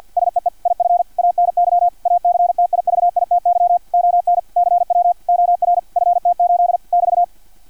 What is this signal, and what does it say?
Morse code, heard as a keyed tone, about 50 wpm, 700 Hz, with radio noise. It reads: SEE IF ANY ACTIVITY ON CW OR FT8 <BT>
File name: archived / CRSnet